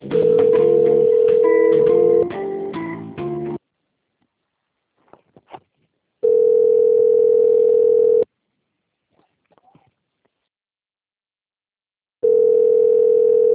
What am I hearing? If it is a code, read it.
Ringback tone (North America), 440 + 480 Hz.